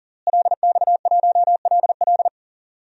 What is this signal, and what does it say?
Morse code, heard as a keyed tone, about 40 wpm, 700 Hz. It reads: LX1LL
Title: DXCC-100 Call Signs - 5 Characters - Repeated 2x - 40wpm